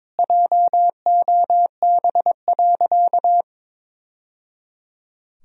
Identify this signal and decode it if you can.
Morse code, heard as a keyed tone, about 22 wpm, 700 Hz. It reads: JOB.
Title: Sentences from Top 1000 Words - Review Part 3 - 22wpm